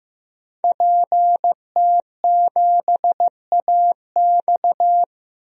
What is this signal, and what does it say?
Morse code, heard as a keyed tone, about 15 wpm, 700 Hz. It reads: PT7AX
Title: DXCC-100 Call Signs - 5 Characters - 15wpm